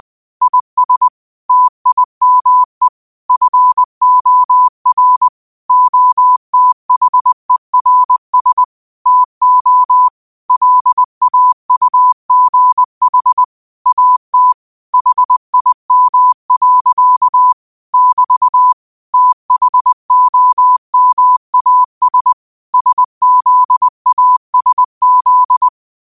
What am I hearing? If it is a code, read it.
Morse code, heard as a keyed tone, about 20 wpm, 1000 Hz. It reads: IS TIME FOR OTHERS TO LAUGH AT HIM. <BT> THOMAS SZASZ